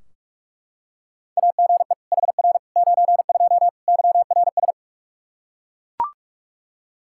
Morse code, heard as a keyed tone, about 41 wpm, 700 Hz. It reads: AGE HR 92 YRS
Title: Practice QSO - Exchange 3 - More Info - 45wpm